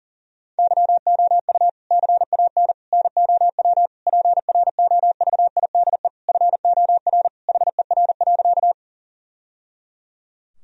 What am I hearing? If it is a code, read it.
Morse code, heard as a keyed tone, about 40 wpm, 700 Hz. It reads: YOU CAN NOW PROVIDE FOR HER.
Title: Sentences from Top 1000 Words - Review Part 3 - 40wpm